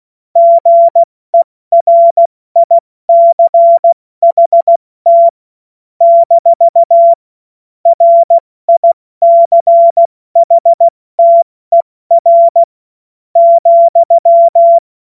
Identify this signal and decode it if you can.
Morse code, heard as a keyed tone, about 16 wpm, spaced as for 13 wpm, 680 Hz. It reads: GERICHT - RICHTER,
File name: Witz-12-August-2025-70-F80